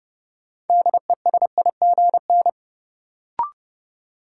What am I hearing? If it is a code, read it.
Morse code, heard as a keyed tone, about 30 wpm, 700 Hz. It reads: DESIGN
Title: Sentences from Top 800 Words - New Word Drill - 30wpm